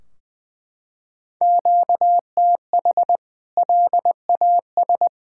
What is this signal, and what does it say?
Morse code, heard as a keyed tone, about 20 wpm, 700 Hz. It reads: QTH LAS
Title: K1USN SST QSO Practice 20wpm